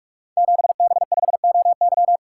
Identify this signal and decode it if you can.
Morse code, heard as a keyed tone, about 44 wpm, 700 Hz. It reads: 7B5OY
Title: Code Groups - 5 Letter:Numbers - 45wpm